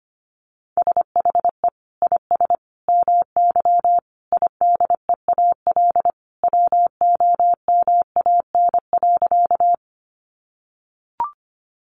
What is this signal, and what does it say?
Morse code, heard as a keyed tone, about 25 wpm, 700 Hz. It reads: SHE IS MY IDEAL WOMAN.
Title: Sentences from Top 1400 Words 25wpm